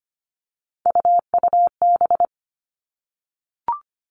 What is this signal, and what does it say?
Morse code, heard as a keyed tone, about 25 wpm, 700 Hz. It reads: UUB
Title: Three Challenging Letter-Number Combinations 25wpm